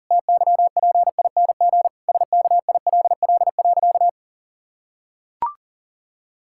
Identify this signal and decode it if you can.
Morse code, heard as a keyed tone, about 40 wpm, 700 Hz. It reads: TYPING SKILL.
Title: Pangram Challenge v2 - 40wpm